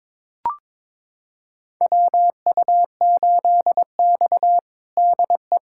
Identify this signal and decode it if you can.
Morse code, heard as a keyed tone, about 22 wpm, 700 Hz. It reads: WU8X DE
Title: USA Callsign DE Callsign - 4 Characters - 22wpm